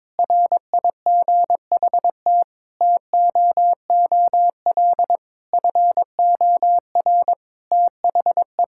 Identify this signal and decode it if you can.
Morse code, heard as a keyed tone, about 22 wpm, 700 Hz. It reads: RIGHT TOOL FOR THE
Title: Sentences from Top 1000 Words - Review Part 6 - 22wpm